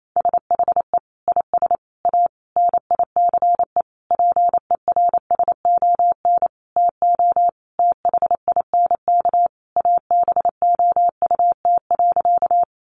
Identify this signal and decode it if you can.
Morse code, heard as a keyed tone, about 28 wpm, 700 Hz. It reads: SHE IS A NICE PERSON TO THINK ABOUT.
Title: Sentences from Top 1100 Words 28wpm